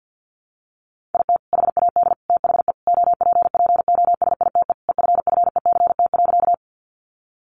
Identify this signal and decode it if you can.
Morse code, heard as a keyed tone, about 50 wpm, 700 Hz. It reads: IT HAD THE OPPOSITE EFFECT.
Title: Sentences from Top 1300 Words 50wpm